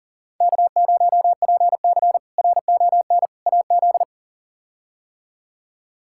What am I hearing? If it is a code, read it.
Morse code, heard as a keyed tone, about 37 wpm, 700 Hz. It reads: K0PC RON AZ
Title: K1USN SST QSO Practice - 40wpm